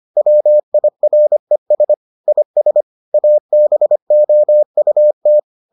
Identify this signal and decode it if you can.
Morse code, heard as a keyed tone, about 25 wpm, 600 Hz. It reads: WIRES IS ABOUT